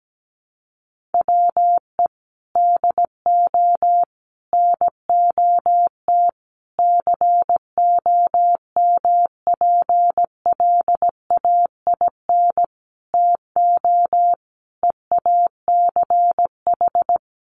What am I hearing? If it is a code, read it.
Morse code, heard as a keyed tone, about 17 wpm, 700 Hz. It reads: WE DO NOT COMPLAIN TO EACH
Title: Sentences from Top 1300 Words 17wpm